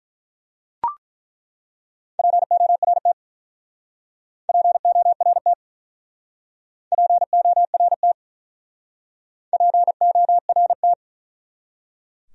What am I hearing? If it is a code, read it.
Morse code, heard as a keyed tone, about 42 wpm, 700 Hz. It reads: PORT PORT PORT PORT
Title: Speed-Racing Top 200 Words 35wpm